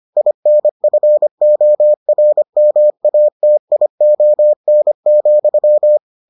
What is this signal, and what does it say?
Morse code, heard as a keyed tone, about 25 wpm, 600 Hz. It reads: INFORMATION,